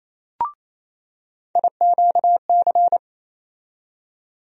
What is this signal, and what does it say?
Morse code, heard as a keyed tone, about 28 wpm, 700 Hz. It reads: IQC
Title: USA Call Sign Postfixes - 3 Letters 28wpm